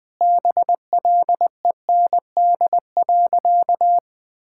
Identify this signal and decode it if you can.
Morse code, heard as a keyed tone, about 20 wpm, 700 Hz. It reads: BLEND.
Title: Sentences with 4-Word Phrases - 20wpm